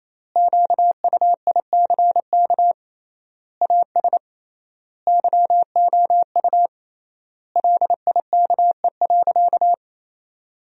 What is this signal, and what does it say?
Morse code, heard as a keyed tone, about 27 wpm, 700 Hz. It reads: QUICK AS YOU LIKE.
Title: 3x Word-Spacing - Sentences from Top 500 Words - 28wpm